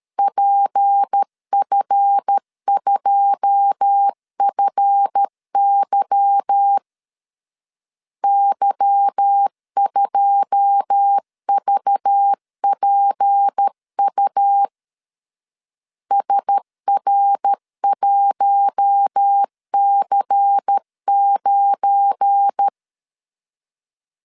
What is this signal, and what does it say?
Morse code, heard as a keyed tone, about 13 wpm, 800 Hz. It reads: PF2FY Y2VPU SR1C9
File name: LEZIONI / 9 2 12wpm 2301